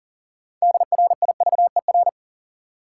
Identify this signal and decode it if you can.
Morse code, heard as a keyed tone, about 40 wpm, 700 Hz. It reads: DRIVER
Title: Six Letter Words - 40wpm